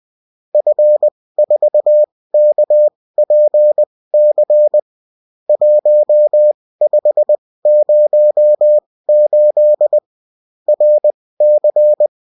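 Morse code, heard as a keyed tone, about 20 wpm, spaced as for 16 wpm, 600 Hz. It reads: F4KPC 1508 RC
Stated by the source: Morse code bulletin of a French amateur CW union, sent as text and in CW.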